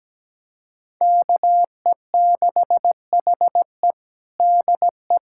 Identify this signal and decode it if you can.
Morse code, heard as a keyed tone, about 17 wpm, 700 Hz. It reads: KE6HE DE
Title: USA Callsign DE Callsign - 5 Characters - 17wpm